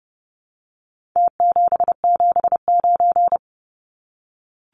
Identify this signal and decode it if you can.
Morse code, heard as a keyed tone, about 30 wpm, 700 Hz. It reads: T779
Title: Numbers - Contest Style - 4-Digit 30wpm